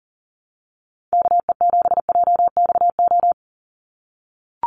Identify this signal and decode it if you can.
Morse code, heard as a keyed tone, about 40 wpm, 700 Hz. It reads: KE7JXO
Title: USA Call Signs 40wpm